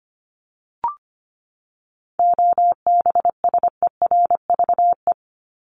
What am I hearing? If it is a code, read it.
Morse code, heard as a keyed tone, about 25 wpm, 700 Hz. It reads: OBSERVE
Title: ED Suffix - Top 100 Words with Base Words 25wpm